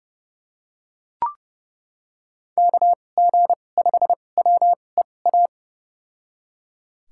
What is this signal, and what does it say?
Morse code, heard as a keyed tone, about 30 wpm, spaced as for 22 wpm, 700 Hz. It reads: KG5WEA
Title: USA Call Signs - 6 Characters - 2x Character Spacing - 30wpm